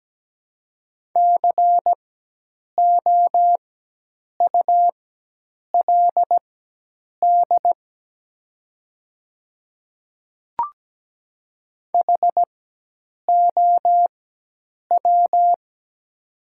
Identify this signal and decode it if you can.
Morse code, heard as a keyed tone, about 17 wpm, spaced as for 8 wpm, 700 Hz. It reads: COULD HOW
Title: 4x Character Spacing - Top 200 Words - 17wpm